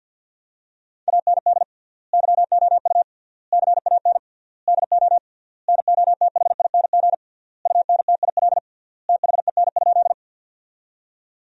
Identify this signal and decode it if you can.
Morse code, heard as a keyed tone, about 50 wpm, 700 Hz. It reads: AND YOU CAN DO NOTHING UNTIL THEN?
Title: 3x Word-Spacing - Sentences from Top 500 Words - 50wpm